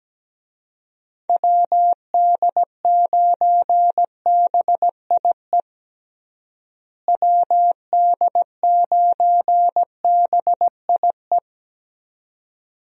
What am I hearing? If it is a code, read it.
Morse code, heard as a keyed tone, about 17 wpm, 700 Hz. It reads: WD9BIE WD9BIE
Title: USA Call Signs - 6 Characters - Repeated 3x - 17wpm